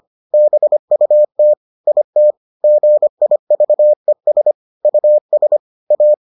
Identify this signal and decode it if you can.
Morse code, heard as a keyed tone, about 25 wpm, 600 Hz. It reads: BUT IT GIVES US A